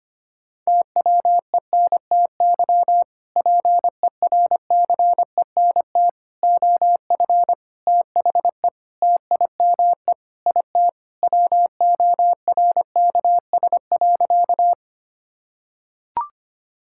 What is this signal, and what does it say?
Morse code, heard as a keyed tone, about 25 wpm, 700 Hz. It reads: TWENTY PERCENT OF THE TIME IT WORKS.
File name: practice / 25wpm / Sentences with 2-Word Phrases - 25wpm